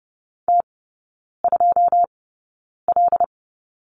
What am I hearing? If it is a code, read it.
Morse code, heard as a keyed tone, about 30 wpm, spaced as for 9 wpm, 700 Hz. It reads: T2L
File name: course / 30:10wpm / 087 Lesson 15e - Number 5 - Three Characters 30wpm